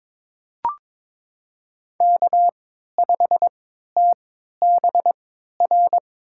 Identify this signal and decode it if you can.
Morse code, heard as a keyed tone, about 22 wpm, spaced as for 13 wpm, 700 Hz. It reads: K5TBR
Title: USA Call Signs - 5 Characters - 3x Character Spacing - 22wpm